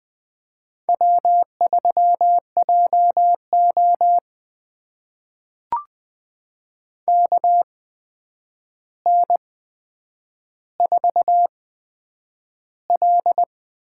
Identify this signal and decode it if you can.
Morse code, heard as a keyed tone, about 20 wpm, spaced as for 9 wpm, 700 Hz. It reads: W3JO KN4L
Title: USA Call Signs - 4 Characters - 8x Character Spacing - 20wpm